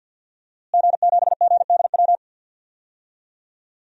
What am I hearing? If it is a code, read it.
Morse code, heard as a keyed tone, about 51 wpm, 700 Hz. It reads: G7GDW
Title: DXCC-100 Call Signs - 5 Characters - Repeated 2x - 50wpm